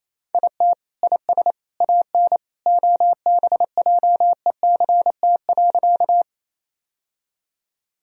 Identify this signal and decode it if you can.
Morse code, heard as a keyed tone, about 27 wpm, 700 Hz. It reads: IT IS AN OBJECT.
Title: Sentences from Top 1000 Words - Review Part 3 - 28wpm